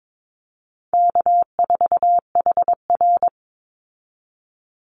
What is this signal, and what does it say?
Morse code, heard as a keyed tone, about 22 wpm, 690 Hz. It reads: K4HR
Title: Easy Call Signs Repeated 2x - 22wpm